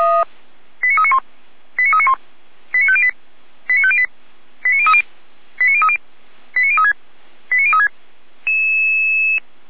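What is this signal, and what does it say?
Five-tone selective call (ZVEI), heard as quick runs of five tones, tones of about 70 ms. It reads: 89391 89391 89508 89508 89030 89030 89036 89036
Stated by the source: ZVEI five-tone selective call received on an SDR.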